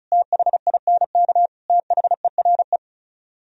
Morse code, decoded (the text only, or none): THINK THERE